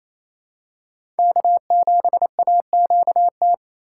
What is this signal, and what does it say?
Morse code, heard as a keyed tone, about 28 wpm, 700 Hz. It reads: K7AQT
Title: USA Call Signs - 5 Characters - 4x Character Spacing - 28wpm